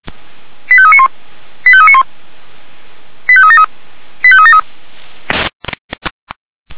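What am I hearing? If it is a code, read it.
Five-tone selective call (ZVEI), heard as quick runs of five tones, tones of about 70 ms. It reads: 85391 85391 85373 85373
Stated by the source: ZVEI five-tone selective call 85751 received on SDR, 464 channel, from fire brigade Feuerwehr Orlamünde audio server.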